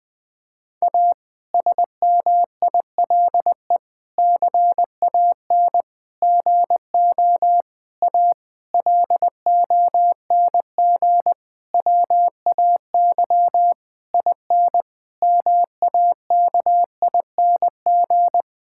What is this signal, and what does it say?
Morse code, heard as a keyed tone, about 20 wpm, 700 Hz. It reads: A SMILE CAN GO A LONG WAY IN MAKING
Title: Difficult Sentences with Hints - 20wpm